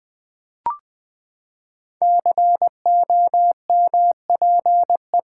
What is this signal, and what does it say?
Morse code, heard as a keyed tone, about 20 wpm, 700 Hz. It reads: COMPE
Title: Build A Word - 5 Letters At A Time - 10-Letter Words - 20wpm